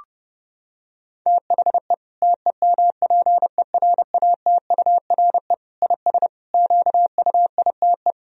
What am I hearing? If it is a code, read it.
Morse code, heard as a keyed tone, about 32 wpm, 700 Hz. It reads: THE TEMPERATURE IS QUITE
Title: Sentences from Top 1000 Words - Review Part 4 - 30wpm